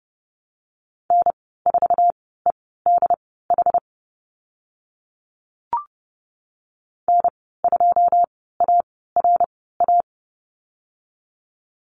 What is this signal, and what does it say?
Morse code, heard as a keyed tone, about 30 wpm, spaced as for 17 wpm, 700 Hz. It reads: N4EDH N2ARA
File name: Lesson 15g - Number 5 - Call Signs 30wpm